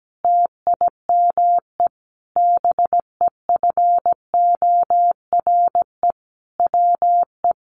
Morse code, heard as a keyed tone, about 17 wpm, 700 Hz. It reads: TIME BEFORE WE
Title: Sentences from Top 1100 Words 17wpm